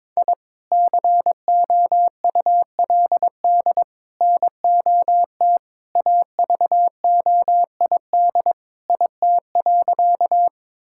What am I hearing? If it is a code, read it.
Morse code, heard as a keyed tone, about 22 wpm, 700 Hz. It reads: I COULD NOT AVOID IT.
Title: Sentences from Top 1000 Words - Review Part 6 - 22wpm